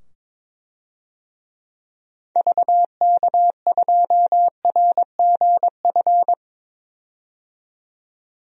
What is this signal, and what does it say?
Morse code, heard as a keyed tone, about 22 wpm, 700 Hz. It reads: VK2RGF
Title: DXCC-100 Call Signs - 6 Characters - 22wpm